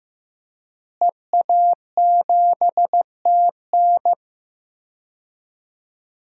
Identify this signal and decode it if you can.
Morse code, heard as a keyed tone, about 15 wpm, 700 Hz. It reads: EA7TN
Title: DXCC-100 Call Signs - 5 Characters - Repeated 2x - 15wpm